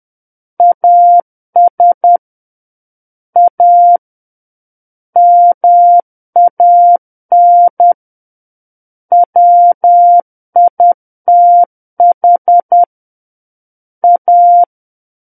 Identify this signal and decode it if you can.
Morse code, heard as a keyed tone, about 10 wpm, 700 Hz. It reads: AS A MAN WITH A
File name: Book One / 01-Chapter 1 - 10 WPM